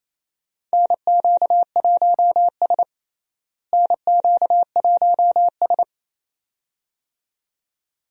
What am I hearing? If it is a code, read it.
Morse code, heard as a keyed tone, about 28 wpm, 700 Hz. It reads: NQ1S NQ1S
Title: USA Call Signs - 4 Characters - Repeated 2x - 28wpm